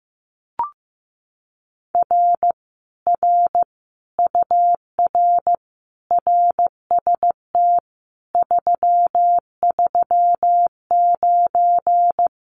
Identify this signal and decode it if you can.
Morse code, heard as a keyed tone, about 15 wpm, 700 Hz. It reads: R R UR RST 339
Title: Common QSO Phrases Encore 15wpm